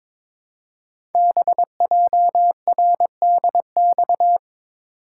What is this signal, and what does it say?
Morse code, heard as a keyed tone, about 22 wpm, 700 Hz. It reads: BJRDX
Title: Code Groups - 5 Letters - 22wpm